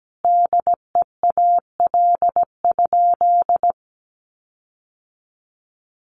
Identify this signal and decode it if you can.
Morse code, heard as a keyed tone, about 17 wpm, 700 Hz. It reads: DEAL?
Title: Sentences from Top 1400 Words 17wpm